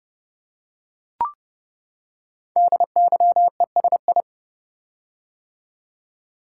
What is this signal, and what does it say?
Morse code, heard as a keyed tone, about 30 wpm, 690 Hz. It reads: DYESI
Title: Code Groups - 5 Letters - 30wpm